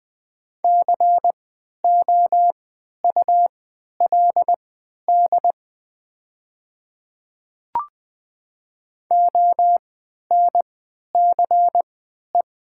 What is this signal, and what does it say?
Morse code, heard as a keyed tone, about 20 wpm, spaced as for 11 wpm, 700 Hz. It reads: COULD ONCE
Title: 3x Character Spacing - Top 300 Words - 20wpm